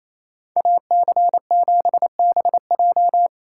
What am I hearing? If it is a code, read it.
Morse code, heard as a keyed tone, about 33 wpm, 700 Hz. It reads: AC7BJ
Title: Speed Racing - USA Call Signs 28wpm